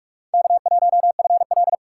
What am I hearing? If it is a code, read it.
Morse code, heard as a keyed tone, about 43 wpm, 700 Hz. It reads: K1FL
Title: USA Call Signs - 4 Characters - 3x Character Spacing - 45wpm